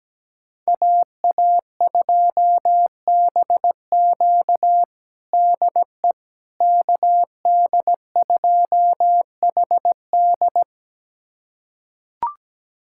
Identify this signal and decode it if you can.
Morse code, heard as a keyed tone, about 17 wpm, 700 Hz. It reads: AA2BQ DE KD2HD
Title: USA Callsign DE Callsign - 5 Characters - 17wpm